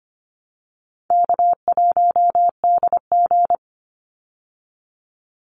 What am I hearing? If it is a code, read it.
Morse code, heard as a keyed tone, about 25 wpm, 700 Hz. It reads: K1DG
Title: Call Signs Repeated 2x - 25wpm